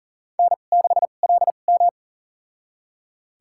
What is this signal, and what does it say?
Morse code, heard as a keyed tone, about 39 wpm, spaced as for 29 wpm, 690 Hz. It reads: N6LM
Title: USA Call Signs - 4 Characters - 2x Character Spacing - 40wpm